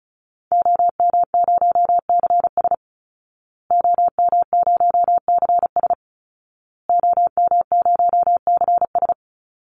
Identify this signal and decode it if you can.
Morse code, heard as a keyed tone, about 34 wpm, 700 Hz. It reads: OM0CS OM0CS OM0CS
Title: Call Signs Repeated 2x - 35wpm